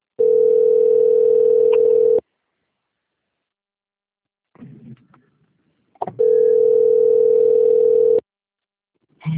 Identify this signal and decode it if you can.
Ringback tone (North America), 440 + 480 Hz.